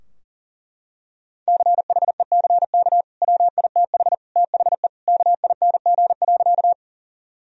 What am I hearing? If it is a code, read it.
Morse code, heard as a keyed tone, about 40 wpm, 700 Hz. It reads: CHECK WITH THE KING.
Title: Sentences from Top 1000 Words - Review Part 3 - 40wpm